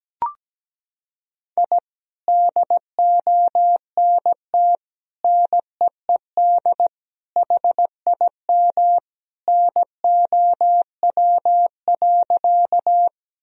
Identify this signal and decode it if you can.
Morse code, heard as a keyed tone, about 17 wpm, 700 Hz. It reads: I DONT NEED HIM NOW.
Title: Sentences from Top 1000 Words - Review Part 1 - 17wpm